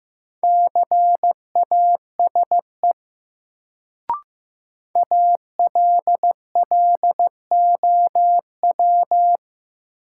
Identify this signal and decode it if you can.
Morse code, heard as a keyed tone, about 15 wpm, 700 Hz. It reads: CASE ALLOW
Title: Sentences from Top 700 Words - New Word Drill - 15wpm